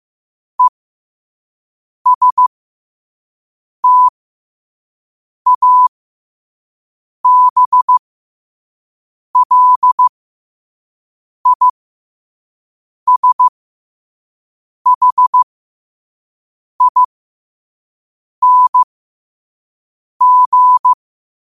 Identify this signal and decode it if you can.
Morse code, heard as a keyed tone, about 15 wpm, spaced as for 5 wpm, 1000 Hz. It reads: ESTABLISHING
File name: qotdfiles / 5wpm